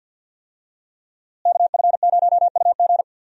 Morse code, heard as a keyed tone, about 50 wpm, 700 Hz. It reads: KF0UG
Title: USA Call Signs - 5 Characters - Repeated 3x - 50wpm